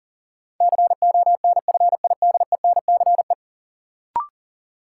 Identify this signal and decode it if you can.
Morse code, heard as a keyed tone, about 40 wpm, 700 Hz. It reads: CONFIDENCE
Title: Sentences from Top 1500 Words - New Word Drill - 40wpm